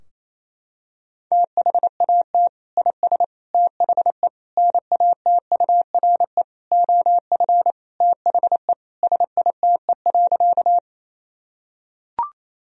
Morse code, heard as a keyed tone, about 27 wpm, 700 Hz. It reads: THAT IS THE NATURE OF THE SITE.
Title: Sentences from Top 1000 Words - Review Part 4 - 28wpm